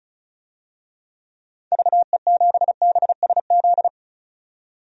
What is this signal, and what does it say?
Morse code, heard as a keyed tone, about 34 wpm, 700 Hz. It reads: VE7BSZ
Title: DXCC-100 Call Signs - 6 Characters - 35wpm